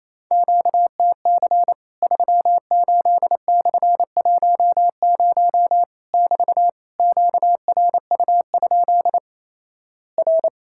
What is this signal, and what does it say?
Morse code, heard as a keyed tone, about 28 wpm, 700 Hz. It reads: QTC 38/10 <BT> QRU? R
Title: Worked All Europe DX Contest - QTC Exchange - 28wpm